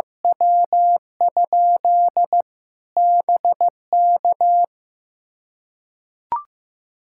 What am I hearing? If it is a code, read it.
Morse code, heard as a keyed tone, about 15 wpm, 700 Hz. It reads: W? BK
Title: Practice QSO - Exchange 1 - Beginning - 15wpm